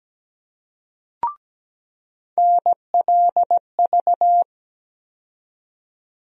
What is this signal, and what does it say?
Morse code, heard as a keyed tone, about 17 wpm, 700 Hz. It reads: NLV
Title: USA Call Sign Postfixes - 3 Letters 17wpm